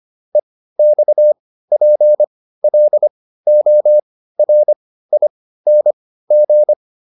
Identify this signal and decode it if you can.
Morse code, heard as a keyed tone, about 25 wpm, spaced as for 15 wpm, 600 Hz. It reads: EXPLORING